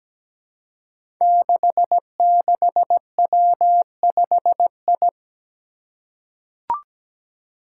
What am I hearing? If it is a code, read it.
Morse code, heard as a keyed tone, about 17 wpm, 700 Hz. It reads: 66W5I
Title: Five Challenging Letter-Number Combinations 17wpm